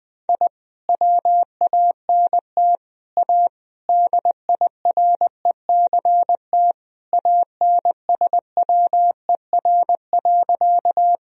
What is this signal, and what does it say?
Morse code, heard as a keyed tone, about 21 wpm, 700 Hz. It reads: I WANT A DIRECT ANSWER.
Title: Sentences from Top 1000 Words - Review Part 2 - 20wpm